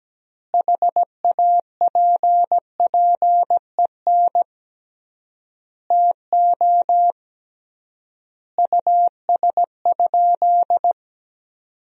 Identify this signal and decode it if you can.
Morse code, heard as a keyed tone, about 17 wpm, 700 Hz. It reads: HAPPEN TO US?
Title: 3x Word-Spacing - Sentences from Top 400 Words - 17wpm